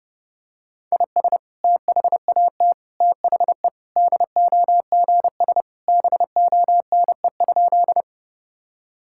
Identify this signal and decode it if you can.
Morse code, heard as a keyed tone, about 32 wpm, 700 Hz. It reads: IS THAT THE DOGS BONE?
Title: Sentences from Top 1600 Words 30wpm